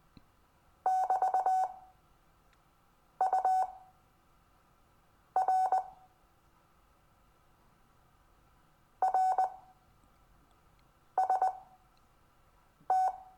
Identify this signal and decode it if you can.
Morse code, heard as a keyed tone, about 19 wpm, spaced as for 5 wpm, 750 Hz. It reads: <BT>UR RST
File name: cw / Lesson22 REVIEW